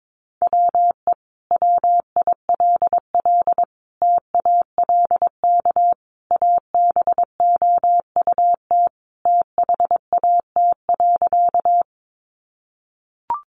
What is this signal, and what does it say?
Morse code, heard as a keyed tone, about 22 wpm, 700 Hz. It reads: WE WILL TALK ABOUT THAT.
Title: Sentences from Top 400 Words 22wpm